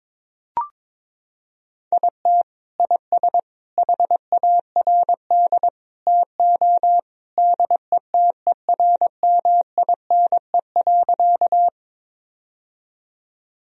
Morse code, heard as a keyed tone, about 21 wpm, 700 Hz. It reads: IT IS HARD TO DETERMINE.
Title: Sentences from Top 1000 Words - Review Part 4 - 22wpm